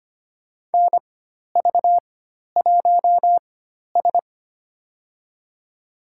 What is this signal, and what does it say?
Morse code, heard as a keyed tone, about 25 wpm, spaced as for 12 wpm, 700 Hz. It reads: NV1S
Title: USA Call Signs - 4 Characters - 4x Character Spacing - 25wpm